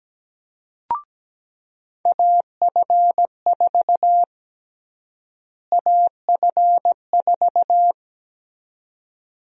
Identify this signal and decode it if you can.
Morse code, heard as a keyed tone, about 17 wpm, 700 Hz. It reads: AF4 AF4
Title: USA Call Sign Prefixes - Repeated 2x - 17wpm